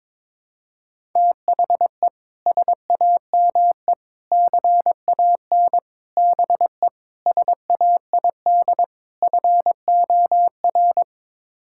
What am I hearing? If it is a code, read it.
Morse code, heard as a keyed tone, about 22 wpm, 700 Hz. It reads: THE SAME CAN BE SAID FOR
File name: practice / 22wpm / Sets of 6 Words - 22wpm